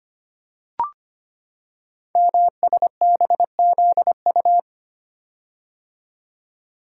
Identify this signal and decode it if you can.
Morse code, heard as a keyed tone, about 25 wpm, 700 Hz. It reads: MSBZU
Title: Code Groups - 5 Letters - 25wpm